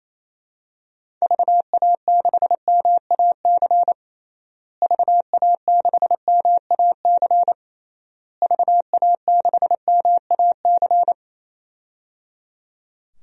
Morse code, heard as a keyed tone, about 28 wpm, 700 Hz. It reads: VA6MAC VA6MAC VA6MAC
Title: DXCC-100 Call Signs - 6 Characters - Repeated 3x - 28wpm